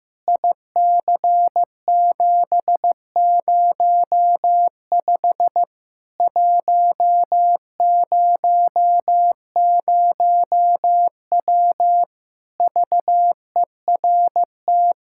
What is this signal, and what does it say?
Morse code, heard as a keyed tone, about 16 wpm, 700 Hz. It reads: IC705 100W VERT
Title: K1USN SST QSO Practice 15wpm